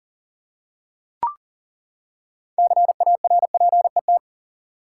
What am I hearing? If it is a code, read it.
Morse code, heard as a keyed tone, about 40 wpm, 700 Hz. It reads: CARPET
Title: Six Letter Words - 40wpm